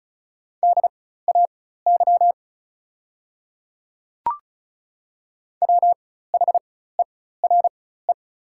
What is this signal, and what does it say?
Morse code, heard as a keyed tone, about 36 wpm, spaced as for 16 wpm, 700 Hz. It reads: DAY WHERE
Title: 4x Character Spacing - Top 200 Words - 35wpm